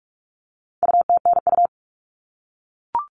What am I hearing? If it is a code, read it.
Morse code, heard as a keyed tone, about 44 wpm, 700 Hz. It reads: UTNU
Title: All Cut Numbers - 4-Digit 45wpm